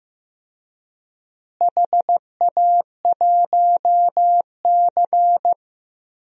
Morse code, heard as a keyed tone, about 15 wpm, 700 Hz. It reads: HA1C
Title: DXCC-100 Call Signs - 4 Characters - Repeated 3x - 15wpm